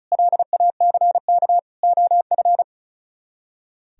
Morse code, heard as a keyed tone, about 43 wpm, 700 Hz. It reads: LACK OF
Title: Speed-Racing - Sets of 3 Words 35wpm